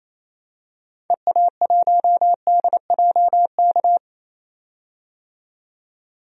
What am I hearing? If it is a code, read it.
Morse code, heard as a keyed tone, about 28 wpm, 700 Hz. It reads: EA1DJK
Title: DXCC-100 Call Signs - 6 Characters - 28wpm